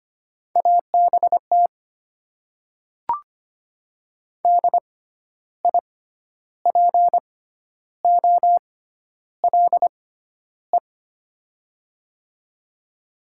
Morse code, heard as a keyed tone, about 25 wpm, spaced as for 11 wpm, 700 Hz. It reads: ABT DIPOLE
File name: Top 90 QSO Elements - 6x Character Spacing - 25wpm